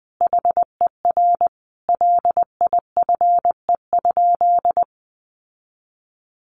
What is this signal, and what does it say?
Morse code, heard as a keyed tone, about 20 wpm, 700 Hz. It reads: HER LIFE?
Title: Sentences from Top 1500 Words 20wpm